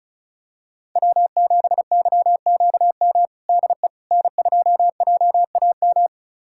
Morse code, heard as a keyed tone, about 34 wpm, 700 Hz. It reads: W7YQM DE N2JAM
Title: USA Callsign DE Callsign - 5 Characters - 35wpm